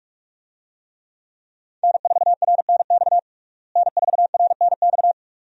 Morse code, heard as a keyed tone, about 46 wpm, 700 Hz. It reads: N4RNX N4RNX
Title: USA Call Signs - 5 Characters - Repeated 2x - 45wpm